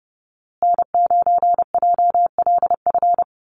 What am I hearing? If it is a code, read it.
Morse code, heard as a keyed tone, about 30 wpm, 700 Hz. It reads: N9JLF
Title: USA Call Signs Repeated 2x 30wpm